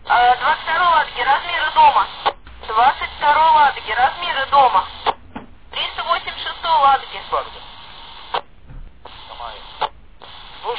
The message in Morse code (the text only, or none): none